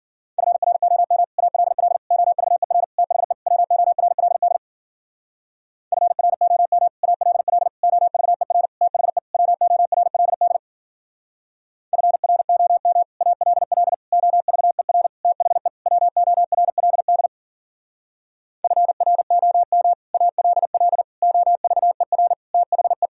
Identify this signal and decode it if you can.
Morse code, heard as a keyed tone, about 49 wpm, 700 Hz. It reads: FROM ALL OVER THE WORLD FROM ALL OVER THE WORLD FROM ALL OVER THE WORLD FROM ALL OVER THE
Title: Speed-Racing - Sets of 5 Words 40wpm